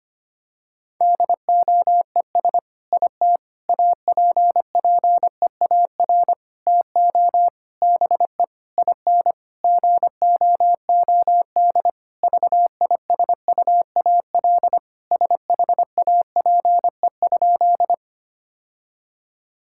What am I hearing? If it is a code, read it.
Morse code, heard as a keyed tone, about 25 wpm, 700 Hz. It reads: DOES IT APPEAR TO BE IN GOOD VISUAL SHAPE?
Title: Sentences from Top 1000 Words - Review Part 6 - 25wpm